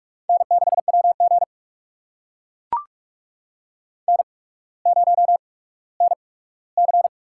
Morse code, heard as a keyed tone, about 43 wpm, spaced as for 19 wpm, 700 Hz. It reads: N6WG N0NC
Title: USA Call Signs - 8x Character Spacing - 45wpm